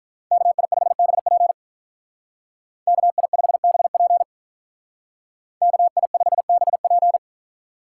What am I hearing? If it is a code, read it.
Morse code, heard as a keyed tone, about 46 wpm, 700 Hz. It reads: KI5BP KI5BP KI5BP